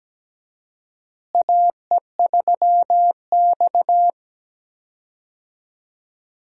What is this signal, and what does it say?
Morse code, heard as a keyed tone, about 17 wpm, 700 Hz. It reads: AE3X
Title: USA Call Signs - 4 Characters - Repeated 2x - 17wpm